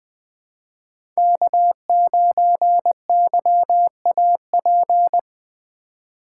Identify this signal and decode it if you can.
Morse code, heard as a keyed tone, about 24 wpm, 700 Hz. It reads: K9YAP
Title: Speed Racing - USA 1x3 Call Signs 20wpm